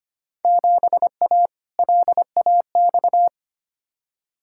Morse code, heard as a keyed tone, about 25 wpm, 700 Hz. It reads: 7A LAX
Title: Field Day Exchanges - 25wpm